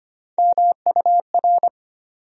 Morse code, heard as a keyed tone, about 25 wpm, 700 Hz. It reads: MUR
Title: USA Call Sign Postfixes - 3 Letters 25wpm